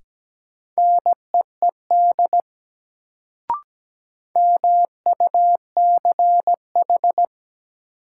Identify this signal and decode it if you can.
Morse code, heard as a keyed tone, about 17 wpm, 700 Hz. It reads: NEED MUCH
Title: Sentences from Top 200 Words - New Word Drill - 17wpm